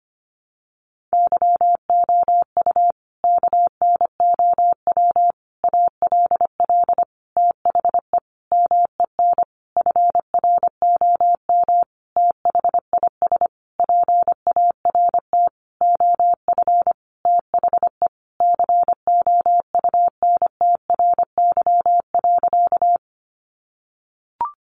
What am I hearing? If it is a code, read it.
Morse code, heard as a keyed tone, about 25 wpm, 700 Hz. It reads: YOU KNOW ALL THE MEN FROM THIS PART OF THE COUNTRY.
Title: Sentences from Top 300 Words 25wpm